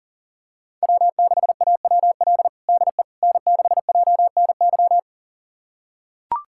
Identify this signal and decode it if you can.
Morse code, heard as a keyed tone, about 40 wpm, 700 Hz. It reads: W6AWL DE N6JNY
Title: USA Callsign DE Callsign - 5 Characters - 40wpm